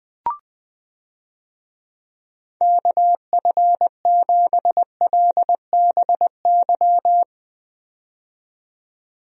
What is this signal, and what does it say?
Morse code, heard as a keyed tone, about 20 wpm, 700 Hz. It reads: KF7LBY
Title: USA Call Signs - 6 Characters - 20wpm